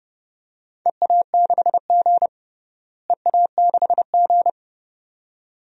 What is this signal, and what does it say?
Morse code, heard as a keyed tone, about 30 wpm, 700 Hz. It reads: EA6G EA6G
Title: DXCC-100 Call Signs - 4 Characters - Repeated 2x - 30wpm